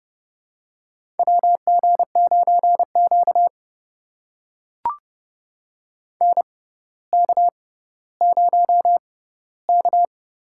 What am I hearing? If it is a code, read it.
Morse code, heard as a keyed tone, about 30 wpm, spaced as for 15 wpm, 690 Hz. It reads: WG9Q NK0K
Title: USA Call Signs - 4 Characters - 6x Character Spacing - 30wpm